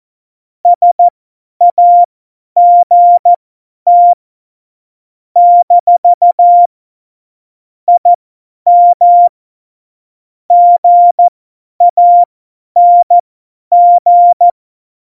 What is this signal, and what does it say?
Morse code, heard as a keyed tone, about 14 wpm, spaced as for 10 wpm, 700 Hz. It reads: SAGT - IM GANG
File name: Morsekurs / Witze / Witz 12. Januar 2021, 50 F70